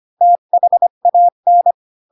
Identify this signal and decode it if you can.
Morse code, heard as a keyed tone, about 25 wpm, 700 Hz. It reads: THAN